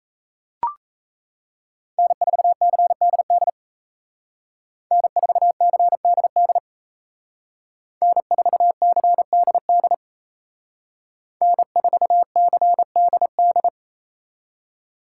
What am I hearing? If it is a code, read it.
Morse code, heard as a keyed tone, about 36 wpm, 700 Hz. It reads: N4CDD N4CDD N4CDD N4CDD